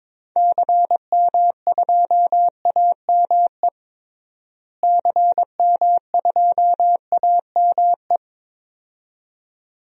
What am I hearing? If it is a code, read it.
Morse code, heard as a keyed tone, about 22 wpm, 700 Hz. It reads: CM2AME CM2AME
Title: DXCC-100 Call Signs - 6 Characters - Repeated 2x - 22wpm